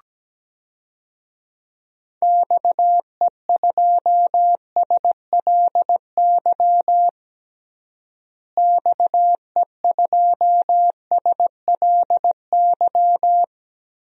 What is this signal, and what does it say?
Morse code, heard as a keyed tone, about 17 wpm, 700 Hz. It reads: XE2SLY XE2SLY
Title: DXCC-100 Call Signs - 6 Characters - Repeated 3x - 17wpm